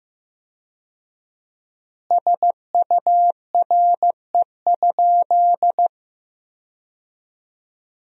Morse code, heard as a keyed tone, about 16 wpm, 700 Hz. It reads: SURE?
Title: 4x Word-Spacing - Sentences from Top 300 Words - 15wpm